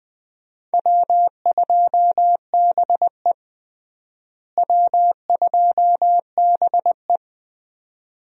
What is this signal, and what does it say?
Morse code, heard as a keyed tone, about 20 wpm, 700 Hz. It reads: W2BE W2BE
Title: USA Call Signs - 4 Characters - Repeated 2x - 20wpm